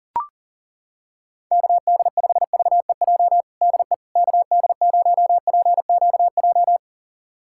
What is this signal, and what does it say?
Morse code, heard as a keyed tone, about 40 wpm, 700 Hz. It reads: KD5VEJ DE KD0PQJ
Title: USA Callsign DE Callsign - 6 Characters - 40wpm